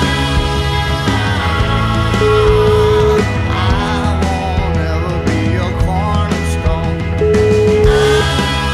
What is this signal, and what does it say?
Ringback tone (Europe, 425 Hz), with noise.